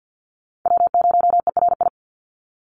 Morse code, heard as a keyed tone, about 48 wpm, 700 Hz. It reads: W0ERI
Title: NA SOTA Chaser Callsigns 50wpm